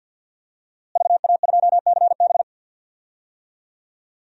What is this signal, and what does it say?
Morse code, heard as a keyed tone, about 53 wpm, 700 Hz. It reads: VA2CB
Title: DXCC-100 Call Signs - 5 Characters - Repeated 2x - 50wpm